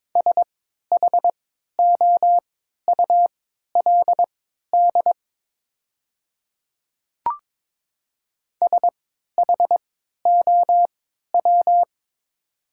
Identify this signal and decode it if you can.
Morse code, heard as a keyed tone, about 22 wpm, spaced as for 12 wpm, 700 Hz. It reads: SHOULD SHOW
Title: 3x Character Spacing - Top 300 Words - 22wpm